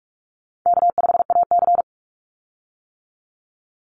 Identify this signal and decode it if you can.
Morse code, heard as a keyed tone, about 44 wpm, 700 Hz. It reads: K5AC
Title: USA Call Signs 45wpm